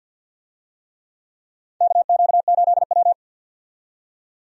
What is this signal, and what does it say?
Morse code, heard as a keyed tone, about 48 wpm, 700 Hz. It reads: KQ8W
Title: USA Call Signs - 4 Characters - 50wpm